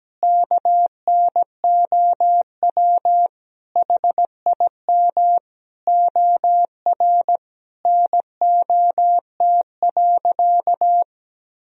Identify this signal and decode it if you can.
Morse code, heard as a keyed tone, about 17 wpm, 700 Hz. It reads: KNOW HIM OR NOT.
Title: Sentences from Top 1000 Words - Review Part 3 - 17wpm